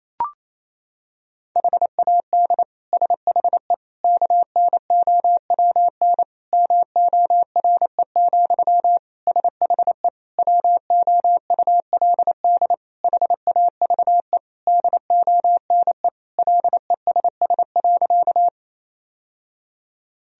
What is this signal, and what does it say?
Morse code, heard as a keyed tone, about 28 wpm, 700 Hz. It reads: HAD SHE KNOWN MORE, SHE WOULD HAVE DONE LESS.
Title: Sentences from Top 1000 Words - Review Part 3 - 28wpm